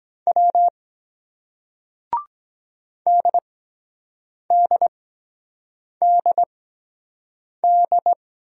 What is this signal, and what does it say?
Morse code, heard as a keyed tone, about 23 wpm, spaced as for 7 wpm, 700 Hz. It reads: W DDDD